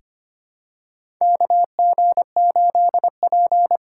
Morse code, heard as a keyed tone, about 25 wpm, 700 Hz. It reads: KG8P
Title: USA Call Signs - 3x Character Spacing - 25wpm